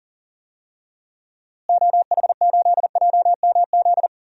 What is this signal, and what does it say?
Morse code, heard as a keyed tone, about 40 wpm, 700 Hz. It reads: OH8JMZ
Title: DXCC-100 Call Signs - 6 Characters - Repeated 2x - 40wpm